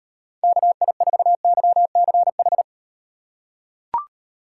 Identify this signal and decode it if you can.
Morse code, heard as a keyed tone, about 38 wpm, 700 Hz. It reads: KI4YCH